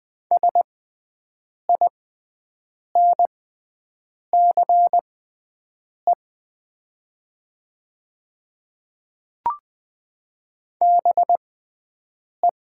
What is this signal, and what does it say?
Morse code, heard as a keyed tone, about 20 wpm, spaced as for 7 wpm, 700 Hz. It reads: SINCE BE
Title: 6x Character Spacing - Top 500 Words - 20wpm